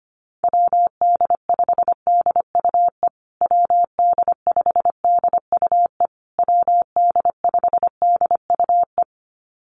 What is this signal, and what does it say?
Morse code, heard as a keyed tone, about 25 wpm, 700 Hz. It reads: WD5DUE WD5DUE WD5DUE
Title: USA Call Signs Repeated 3x 25wpm